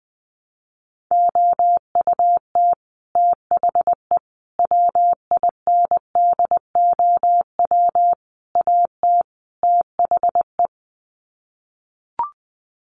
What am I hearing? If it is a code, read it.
Morse code, heard as a keyed tone, about 19 wpm, 700 Hz. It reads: OUT THE WINDOW AT THE
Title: Sets of 5 Words Encore2 20wpm